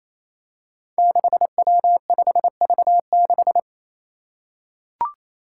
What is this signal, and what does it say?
Morse code, heard as a keyed tone, about 27 wpm, 700 Hz. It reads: 6W5V6
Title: Five Challenging Letter-Number Combinations 28wpm